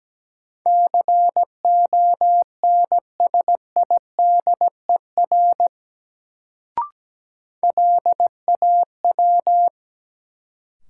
Morse code, heard as a keyed tone, about 17 wpm, 700 Hz. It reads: CONSIDER LAW
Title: Sentences from Top 600 Words - New Word Drill - 17wpm